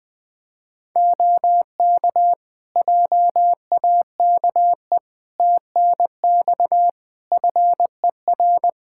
Morse code, heard as a keyed tone, about 20 wpm, 700 Hz. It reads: OK JAKE TNX FER
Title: Practice QSO - Exchange 4 - Ending - 20wpm